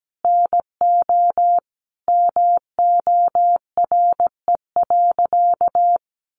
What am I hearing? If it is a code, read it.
Morse code, heard as a keyed tone, about 17 wpm, 700 Hz. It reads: NO MORE.
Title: Sentences from Top 1100 Words 17wpm